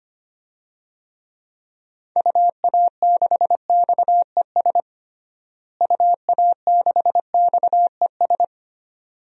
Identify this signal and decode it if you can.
Morse code, heard as a keyed tone, about 25 wpm, 700 Hz. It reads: UA6XES UA6XES
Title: DXCC-100 Call Signs - 6 Characters - Repeated 2x - 25wpm